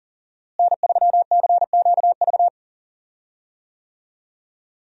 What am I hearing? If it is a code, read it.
Morse code, heard as a keyed tone, about 40 wpm, 700 Hz. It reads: N3CQV
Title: USA Call Signs - 5 Characters - Repeated 2x - 40wpm